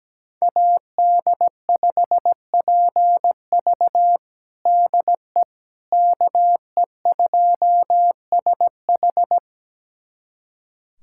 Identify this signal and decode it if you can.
Morse code, heard as a keyed tone, about 17 wpm, 700 Hz. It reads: AD5PV DE KE2SH
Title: USA Callsign DE Callsign - 5 Characters - 17wpm